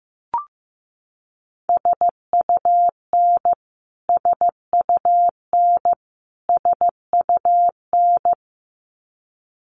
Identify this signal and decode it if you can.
Morse code, heard as a keyed tone, about 15 wpm, 700 Hz. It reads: SUN SUN SUN
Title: Top 500 Words Repeated 15wpm